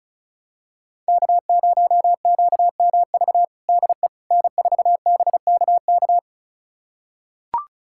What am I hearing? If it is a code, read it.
Morse code, heard as a keyed tone, about 36 wpm, 700 Hz. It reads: K0QMV DE N4BKK
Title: USA Callsign DE Callsign - 5 Characters - 35wpm